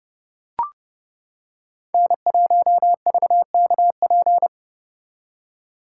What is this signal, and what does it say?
Morse code, heard as a keyed tone, about 29 wpm, 700 Hz. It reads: N1VKP